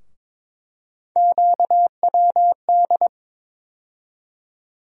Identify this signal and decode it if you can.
Morse code, heard as a keyed tone, about 22 wpm, 700 Hz. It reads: QWD
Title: USA Call Sign Postfixes - 3 Letters 22wpm